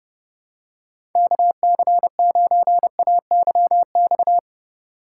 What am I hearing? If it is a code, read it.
Morse code, heard as a keyed tone, about 31 wpm, 700 Hz. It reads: KC9AYX
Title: USA Call Signs - 6 Characters - 2x Character Spacing - 30wpm